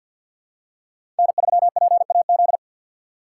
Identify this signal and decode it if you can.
Morse code, heard as a keyed tone, about 53 wpm, 700 Hz. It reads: N3PAZ
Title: USA Call Signs - 5 Characters - 3x Character Spacing - 50wpm